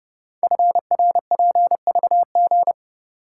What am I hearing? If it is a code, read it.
Morse code, heard as a keyed tone, about 29 wpm, 700 Hz. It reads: FRPVG
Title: Code Groups - 5 Letters - 30wpm